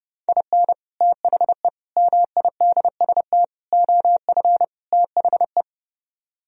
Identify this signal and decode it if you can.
Morse code, heard as a keyed tone, about 30 wpm, 700 Hz. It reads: IN THE MIDST OF THE